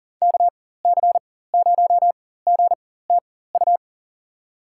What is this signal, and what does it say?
Morse code, heard as a keyed tone, about 40 wpm, spaced as for 19 wpm, 700 Hz. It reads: KC0GTU
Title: USA Call Signs - 6 Characters - 4x Character Spacing - 40wpm